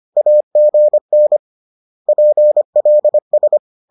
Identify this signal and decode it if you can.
Morse code, heard as a keyed tone, about 25 wpm, 600 Hz. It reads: AGN PLS